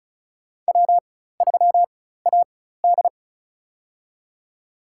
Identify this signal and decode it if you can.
Morse code, heard as a keyed tone, about 36 wpm, spaced as for 16 wpm, 700 Hz. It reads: W3AD
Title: USA Call Signs - 4 Characters - 4x Character Spacing - 35wpm